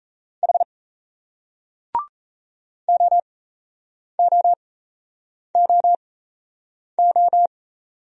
Morse code, heard as a keyed tone, about 38 wpm, spaced as for 8 wpm, 700 Hz. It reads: H OOOO